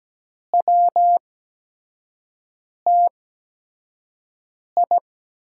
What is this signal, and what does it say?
Morse code, heard as a keyed tone, about 17 wpm, spaced as for 5 wpm, 700 Hz. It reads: WTI
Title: USA Call Signs - 6 Characters - 8x Character Spacing - 17wpm